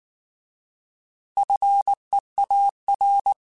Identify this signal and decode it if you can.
Morse code, heard as a keyed tone, about 19 wpm, 800 Hz. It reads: FEAR